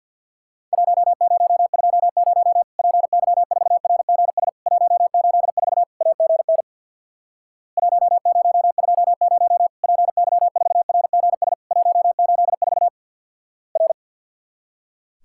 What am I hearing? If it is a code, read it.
Morse code, heard as a keyed tone, about 50 wpm, 700 Hz. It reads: 1020 PY4RGS 184 AGN 1020 PY4RGS 184 R
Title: Worked All Europe DX Contest - QTC Exchange with Repeats - 50wpm